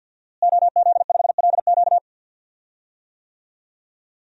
Morse code, heard as a keyed tone, about 52 wpm, 700 Hz. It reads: OZ5LQ
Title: DXCC-100 Call Signs - 5 Characters - Repeated 3x - 50wpm